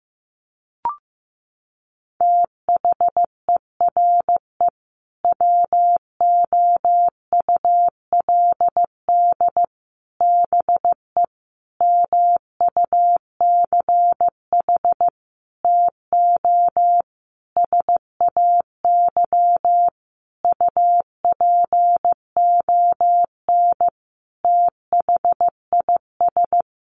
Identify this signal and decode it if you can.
Morse code, heard as a keyed tone, about 15 wpm, 700 Hz. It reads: THERE WOULD BE MUCH TO SAY UPON THIS
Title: Sentences from Top 1200 Words 15wpm